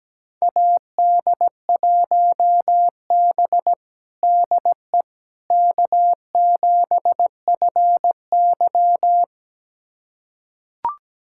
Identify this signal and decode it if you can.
Morse code, heard as a keyed tone, about 17 wpm, 700 Hz. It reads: AD1B DE K7FY
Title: USA Callsign DE Callsign - 4 Characters - 17wpm